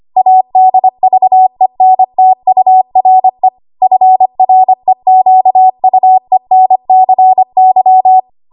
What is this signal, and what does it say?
Morse code, heard as a keyed tone, about 25 wpm, 750 Hz. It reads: ADVENTURE FREQUENCY
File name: Morse / 25WPM